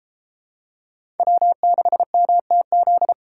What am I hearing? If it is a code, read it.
Morse code, heard as a keyed tone, about 32 wpm, 700 Hz. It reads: W6MTZ